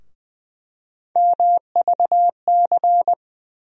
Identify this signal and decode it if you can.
Morse code, heard as a keyed tone, about 20 wpm, 700 Hz. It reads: MVC
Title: USA Call Sign Postfixes - 3 Letters 20wpm